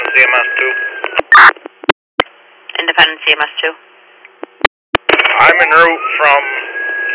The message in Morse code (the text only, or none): none